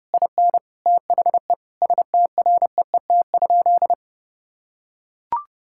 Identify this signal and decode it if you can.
Morse code, heard as a keyed tone, about 32 wpm, 700 Hz. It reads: IN THE STREET?
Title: Sentences from Top 1600 Words 30wpm